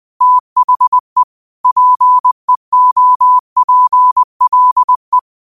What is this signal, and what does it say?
Morse code, heard as a keyed tone, about 20 wpm, 1000 Hz. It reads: THE PEOPLE